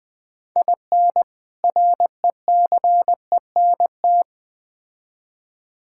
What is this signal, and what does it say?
Morse code, heard as a keyed tone, about 20 wpm, 700 Hz. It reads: IN RECENT
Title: Build A Phrase - 3-Word Phrases - 20wpm